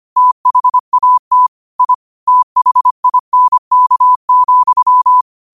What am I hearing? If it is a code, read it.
Morse code, heard as a keyed tone, about 25 wpm, 1000 Hz. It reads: THAT I THINK,